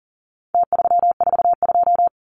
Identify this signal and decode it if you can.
Morse code, heard as a keyed tone, about 40 wpm, 700 Hz. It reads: T342
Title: Numbers - Contest Style - 4-Digit 40wpm